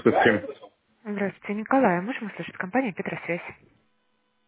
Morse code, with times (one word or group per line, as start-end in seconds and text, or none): none